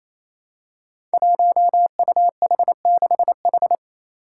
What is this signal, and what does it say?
Morse code, heard as a keyed tone, about 28 wpm, 700 Hz. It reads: 1UH6H
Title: Five Challenging Letter-Number Combinations 28wpm